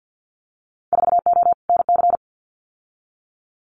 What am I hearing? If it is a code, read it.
Morse code, heard as a keyed tone, about 50 wpm, 700 Hz. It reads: 4O NC
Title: Winter Field Day Exchanges 50wpm